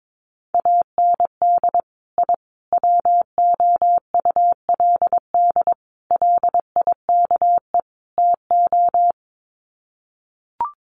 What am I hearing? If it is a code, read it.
Morse code, heard as a keyed tone, about 22 wpm, 700 Hz. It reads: AND I WOULD LIKE TO
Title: Sets of 5 Words Encore 22wpm